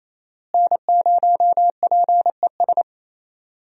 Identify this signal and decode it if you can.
Morse code, heard as a keyed tone, about 28 wpm, 700 Hz. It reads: N0PES
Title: USA Call Signs - 5 Characters - Repeated 3x - 28wpm